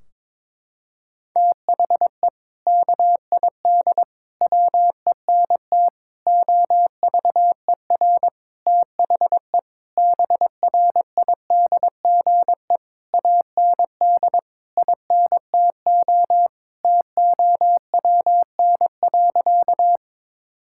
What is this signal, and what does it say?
Morse code, heard as a keyed tone, about 22 wpm, 700 Hz. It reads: THE KID WENT OVER THE BRIDGE AND INTO TOWN.
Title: Sentences from Top 1600 Words 22wpm